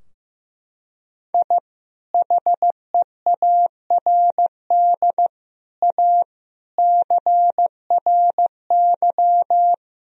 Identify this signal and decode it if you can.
Morse code, heard as a keyed tone, about 15 wpm, 700 Hz. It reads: I HEARD A CRY
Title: Sentences from Top 1000 Words - Review Part 2 - 15wpm